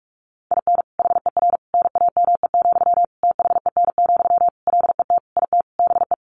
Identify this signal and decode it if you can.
Morse code, heard as a keyed tone, about 44 wpm, 700 Hz. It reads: IN HER NAME, THEN, LET IT BE
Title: Sentences from Top 500 Words 45wpm